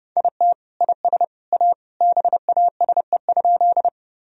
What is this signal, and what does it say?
Morse code, heard as a keyed tone, about 32 wpm, 700 Hz. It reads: IT IS A BASE?
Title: Sentences from Top 1000 Words - Review Part 1 - 30wpm